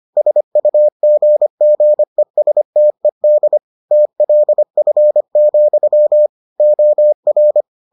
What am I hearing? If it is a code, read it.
Morse code, heard as a keyed tone, about 25 wpm, 600 Hz. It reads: SUGGESTED TLF, OR